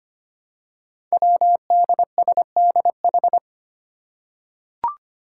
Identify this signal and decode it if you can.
Morse code, heard as a keyed tone, about 25 wpm, 700 Hz. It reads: WDSDH
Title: Five Challenging Letter-Number Combinations 25wpm